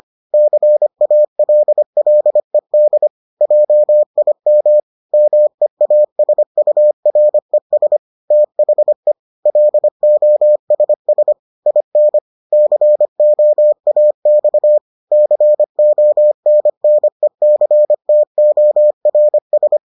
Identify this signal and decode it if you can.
Morse code, heard as a keyed tone, about 25 wpm, 600 Hz. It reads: CALLED JIM MEASURES THE LOSS IN COAX CONNECTORS